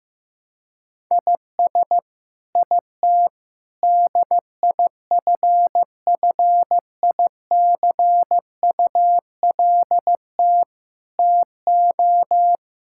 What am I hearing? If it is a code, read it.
Morse code, heard as a keyed tone, about 15 wpm, 700 Hz. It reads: IS IT DIFFICULT TO
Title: Sentences from Top 1000 Words - Review Part 5 - 15wpm